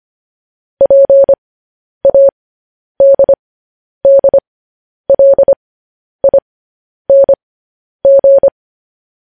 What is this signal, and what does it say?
Morse code, heard as a keyed tone, about 25 wpm, spaced as for 10 wpm, 550 Hz. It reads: PADDLING
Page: News Headlines in Morse Code at 10 WPM